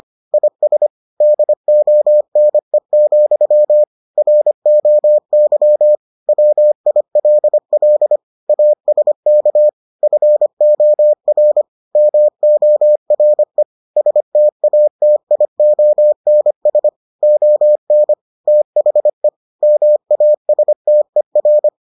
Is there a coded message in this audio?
Yes: Morse code from 0.3 to 21.7 s, heard as a keyed tone, about 25 wpm, 600 Hz. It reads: IS DONE, ROY WILL ASK FOR MORE STATIONS ON THE MASTER